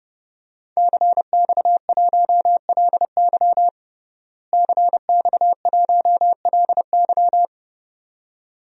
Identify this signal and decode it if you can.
Morse code, heard as a keyed tone, about 29 wpm, 700 Hz. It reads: CX1LY CX1LY
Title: DXCC-100 Call Signs - 5 Characters - Repeated 2x - 30wpm